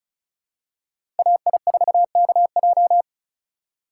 Morse code, heard as a keyed tone, about 36 wpm, 700 Hz. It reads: AI4KJ
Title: USA Call Signs - 5 Characters - 4x Character Spacing - 35wpm